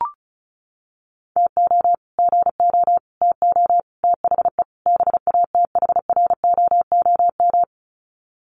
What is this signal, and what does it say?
Morse code, heard as a keyed tone, about 34 wpm, 700 Hz. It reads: TO GO TO THE BATHROOM
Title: Sets of 5 Words Encore 35wpm